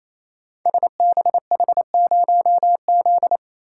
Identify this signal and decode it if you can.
Morse code, heard as a keyed tone, about 27 wpm, 700 Hz. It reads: SBH0Z
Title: Code Groups - 5 Letter:Numbers - 28wpm